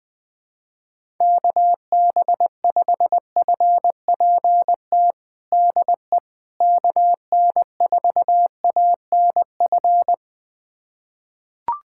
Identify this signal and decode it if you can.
Morse code, heard as a keyed tone, about 20 wpm, 700 Hz. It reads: KB5FPT DE KN4ANF
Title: USA Callsign DE Callsign - 6 Characters - 20wpm